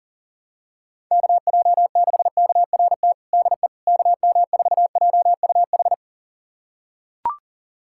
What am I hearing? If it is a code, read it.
Morse code, heard as a keyed tone, about 40 wpm, 700 Hz. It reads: KJ6KRT DE KM4JUH
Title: USA Callsign DE Callsign 40wpm